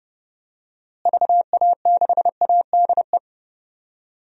Morse code, heard as a keyed tone, about 30 wpm, 700 Hz. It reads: VA6ADE
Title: DXCC-100 Call Signs - 6 Characters - Repeated 2x - 30wpm